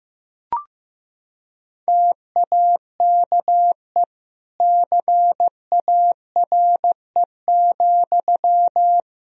Morse code, heard as a keyed tone, about 14 wpm, 700 Hz. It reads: TAKE CARE,
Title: Sentences from Top 1000 Words - Review Part 4 - 15wpm